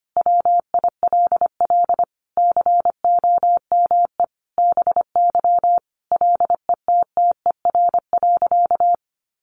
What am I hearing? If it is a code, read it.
Morse code, heard as a keyed tone, about 25 wpm, 700 Hz. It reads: WILL COME BY LETTER.
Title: Sentences from Top 1400 Words 25wpm